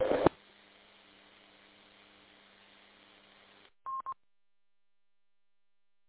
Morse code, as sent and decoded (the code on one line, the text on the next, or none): none